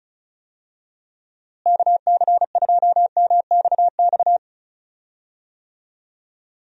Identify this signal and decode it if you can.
Morse code, heard as a keyed tone, about 34 wpm, 700 Hz. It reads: KC2MXX
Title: USA Call Signs - 6 Characters - Repeated 3x - 35wpm